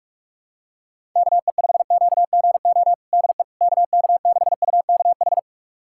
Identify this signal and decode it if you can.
Morse code, heard as a keyed tone, about 41 wpm, 700 Hz. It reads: KE5QGO DE KK6UKH
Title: USA Callsign DE Callsign - 6 Characters - 45wpm